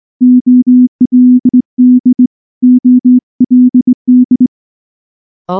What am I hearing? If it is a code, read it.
Morse code, heard as a keyed tone, about 20 wpm, 250 Hz. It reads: OLD OLD
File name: morse / morse-training-20WPM-7